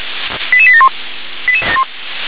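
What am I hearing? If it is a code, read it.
Five-tone selective call (ZVEI), heard as quick runs of five tones, tones of about 70 ms. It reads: 88871
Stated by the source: ZVEI five-tone selective call 88747, received by SDR.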